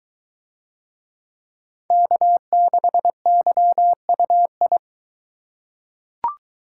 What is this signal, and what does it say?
Morse code, heard as a keyed tone, about 22 wpm, 700 Hz. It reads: K6YUI